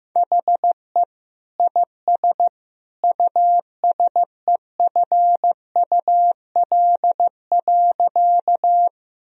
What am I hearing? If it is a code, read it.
Morse code, heard as a keyed tone, about 16 wpm, 700 Hz. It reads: HE IS USEFUL.
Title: Sentences from Top 1000 Words - Review Part 4 - 15wpm